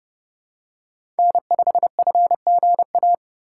Morse code, heard as a keyed tone, about 28 wpm, 700 Hz. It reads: N5FGA
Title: USA Call Signs - 5 Characters - 3x Character Spacing - 30wpm